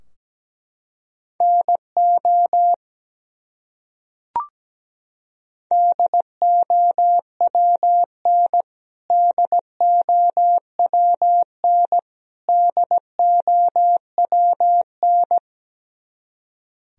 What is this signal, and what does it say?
Morse code, heard as a keyed tone, about 17 wpm, 700 Hz. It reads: NO DOWN DOWN DOWN
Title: Top 100 Words Repeated 17wpm